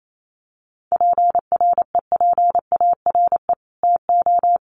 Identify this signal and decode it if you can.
Morse code, heard as a keyed tone, about 28 wpm, 700 Hz. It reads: PREPARE TO
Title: Sentences from Top 900 Words 28wpm